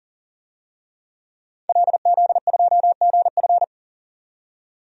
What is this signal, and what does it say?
Morse code, heard as a keyed tone, about 40 wpm, 700 Hz. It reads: LZ2GF
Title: DXCC-100 Call Signs - 5 Characters - 40wpm